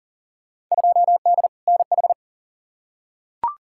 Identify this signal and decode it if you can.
Morse code, heard as a keyed tone, about 40 wpm, 700 Hz. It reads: 2D NH
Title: Field Day Exchanges Encore - 40wpm